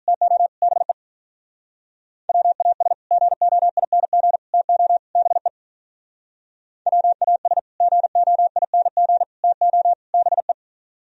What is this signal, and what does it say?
Morse code, heard as a keyed tone, about 42 wpm, 700 Hz. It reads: TO BE WAS GOING TO BE WAS GOING TO BE
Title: Speed-Racing - Sets of 4 Words 35wpm